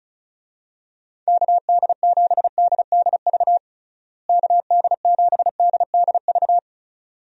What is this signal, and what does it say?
Morse code, heard as a keyed tone, about 34 wpm, 700 Hz. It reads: KD7DDV KD7DDV
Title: USA Call Signs - 6 Characters - Repeated 2x - 35wpm